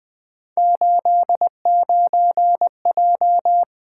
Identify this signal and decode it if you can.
Morse code, heard as a keyed tone, about 24 wpm, 700 Hz. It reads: 89J
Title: Speed-Racing - Three Letter-Number Combinations - 20wpm